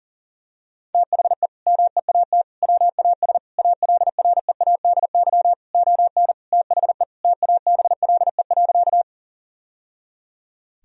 Morse code, heard as a keyed tone, about 37 wpm, 700 Hz. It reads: THE MEAT WAS ALREADY ON THE TABLE.
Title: Sentences from Top 1000 Words - Review Part 3 - 40wpm